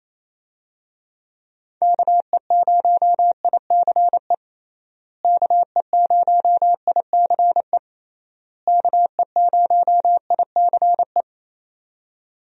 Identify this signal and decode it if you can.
Morse code, heard as a keyed tone, about 28 wpm, 700 Hz. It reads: KE0ICE KE0ICE KE0ICE
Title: USA Call Signs - 6 Characters - Repeated 3x - 28wpm